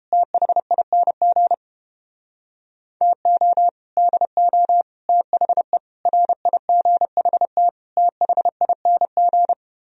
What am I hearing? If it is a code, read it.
Morse code, heard as a keyed tone, about 29 wpm, 700 Hz. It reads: THING TO DO THE RIGHT THING